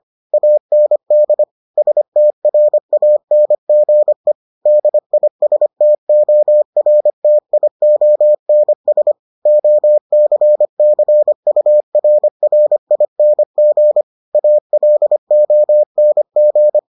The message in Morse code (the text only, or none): AND STRANGE DISTORTIONS OCCURRING ALONG